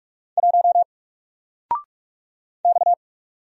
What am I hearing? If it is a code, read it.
Morse code, heard as a keyed tone, about 46 wpm, 700 Hz. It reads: 1 X
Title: Single Character (Letter:Number:Punctuation:Prosign) 45wpm